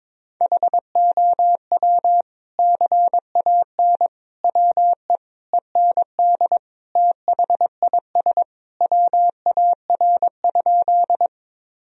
Morse code, heard as a keyed tone, about 22 wpm, 700 Hz. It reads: HOW CAN WE END THIS WAR?
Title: Sentences from Top 1000 Words - Review Part 2 - 22wpm